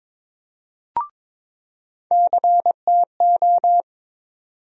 Morse code, heard as a keyed tone, about 22 wpm, 700 Hz. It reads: CTO
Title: USA Call Sign Postfixes - 3 Letters 22wpm